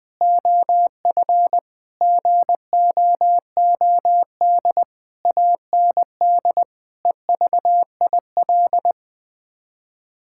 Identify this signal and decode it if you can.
Morse code, heard as a keyed tone, about 21 wpm, 700 Hz. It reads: OF GOOD AND EVIL
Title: Sets of 9 Words - 20wpm